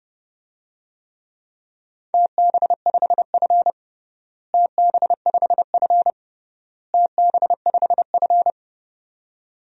Morse code, heard as a keyed tone, about 32 wpm, 700 Hz. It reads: TB5F TB5F TB5F
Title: DXCC-100 Call Signs - 4 Characters - Repeated 3x - 30wpm